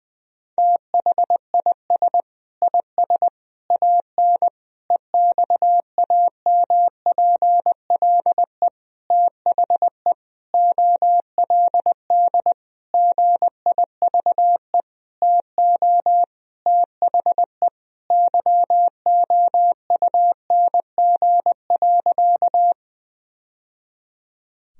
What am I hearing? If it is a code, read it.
Morse code, heard as a keyed tone, about 20 wpm, 700 Hz. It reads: THIS IS AN EXAMPLE THE OLD GIVE TO THE YOUNG.
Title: Sentences from Top 1000 Words - Review Part 1 - 20wpm